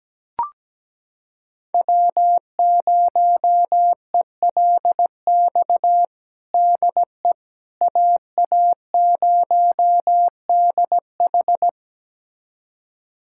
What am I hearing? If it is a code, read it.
Morse code, heard as a keyed tone, about 17 wpm, 700 Hz. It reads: W0ELX DE AA0DH
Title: USA Callsign DE Callsign - 5 Characters - 17wpm